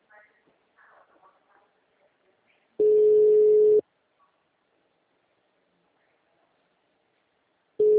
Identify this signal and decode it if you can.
Ringback tone (Europe, 425 Hz).